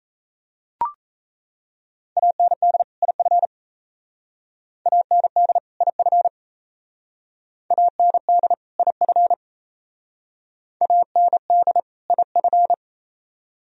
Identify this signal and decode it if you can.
Morse code, heard as a keyed tone, about 37 wpm, 700 Hz. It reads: AND IF AND IF AND IF AND IF